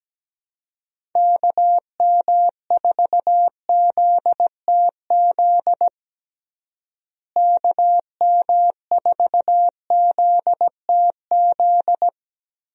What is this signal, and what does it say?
Morse code, heard as a keyed tone, about 17 wpm, 700 Hz. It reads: KM4ZTZ KM4ZTZ
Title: USA Call Signs - 6 Characters - Repeated 3x - 17wpm